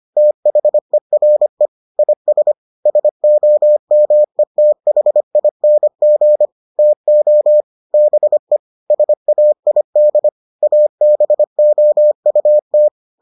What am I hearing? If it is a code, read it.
Morse code, heard as a keyed tone, about 25 wpm, 600 Hz. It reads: THERE IS SOMETHING TO BE SAID ABOUT